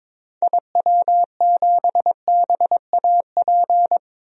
Morse code, heard as a keyed tone, about 22 wpm, 700 Hz. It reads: IW7BAP
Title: DXCC-100 Call Signs - 6 Characters - Repeated 3x - 22wpm